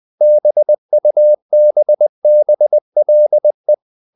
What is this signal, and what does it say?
Morse code, heard as a keyed tone, about 20 wpm, 600 Hz. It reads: BUBBLE
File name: the old man and the sea / the old man and the sea part 1 20wpm